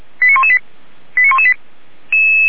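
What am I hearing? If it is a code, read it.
Five-tone selective call (ZVEI), heard as quick runs of five tones, tones of about 70 ms. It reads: 89118 89118
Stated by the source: ZVEI five-tone call received on SDR.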